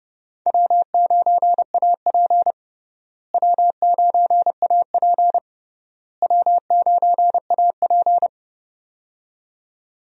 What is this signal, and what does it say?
Morse code, heard as a keyed tone, about 30 wpm, 700 Hz. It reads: W9AP W9AP W9AP
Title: USA Call Signs - 4 Characters - Repeated 3x - 30wpm